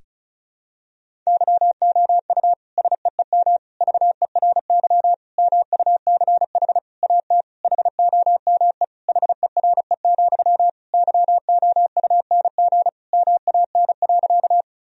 Morse code, heard as a keyed tone, about 34 wpm, 700 Hz. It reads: YOU SEEM VERY MUCH AT HOME HERE, YOUNG MAN.
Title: Sentences from Top 1000 Words - Review Part 1 - 35wpm